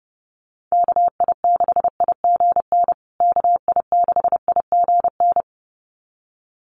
Morse code, heard as a keyed tone, about 30 wpm, 700 Hz. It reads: KI6IGN KI6IGN
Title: USA Call Signs Repeated 2x 30wpm